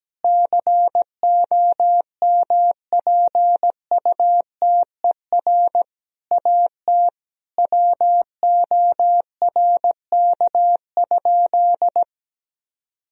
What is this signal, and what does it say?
Morse code, heard as a keyed tone, about 17 wpm, 700 Hz. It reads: COMPUTER AT WORK?
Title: Sentences from Top 1000 Words - Review Part 3 - 17wpm